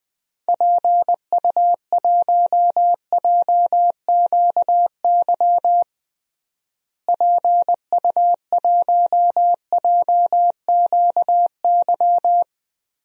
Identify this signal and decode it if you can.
Morse code, heard as a keyed tone, about 21 wpm, 700 Hz. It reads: PU1JQY PU1JQY
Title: DXCC-100 Call Signs - 6 Characters - Repeated 3x - 20wpm